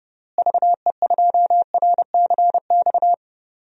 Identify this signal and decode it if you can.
Morse code, heard as a keyed tone, about 32 wpm, 700 Hz. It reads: VE2RCX
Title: DXCC-100 Call Signs - 6 Characters - Repeated 3x - 30wpm